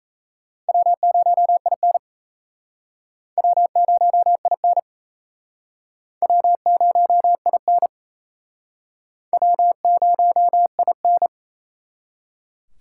Morse code, heard as a keyed tone, about 37 wpm, 700 Hz. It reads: W0IN W0IN W0IN W0IN